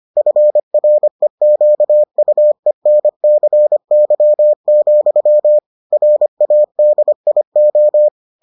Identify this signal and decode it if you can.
Morse code, heard as a keyed tone, about 25 wpm, 600 Hz. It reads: FREQUENCY, RADIO